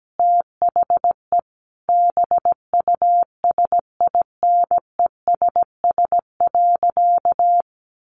Morse code, heard as a keyed tone, about 17 wpm, 700 Hz. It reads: THE BUSINESS.
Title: Sentences from Top 1400 Words 17wpm